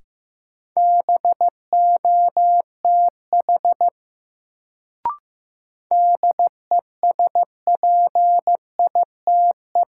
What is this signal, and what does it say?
Morse code, heard as a keyed tone, about 15 wpm, 700 Hz. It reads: BOTH DESPITE
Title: Sets of 4 Words - New Word Drill - 15wpm